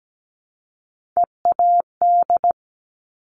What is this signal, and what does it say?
Morse code, heard as a keyed tone, about 17 wpm, 700 Hz. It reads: EAD
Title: Top Three Letter Combinations in Words 17wpm